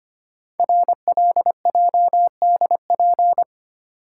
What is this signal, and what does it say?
Morse code, heard as a keyed tone, about 26 wpm, 700 Hz. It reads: RLJDP
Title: Code Groups - 5 Letters - 25wpm